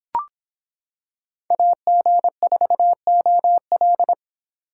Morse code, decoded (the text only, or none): AG4OL